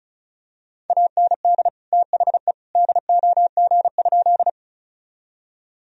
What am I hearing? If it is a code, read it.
Morse code, heard as a keyed tone, about 34 wpm, 700 Hz. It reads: AND THE DOG?
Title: Sentences from Top 1000 Words - Review Part 2 - 35wpm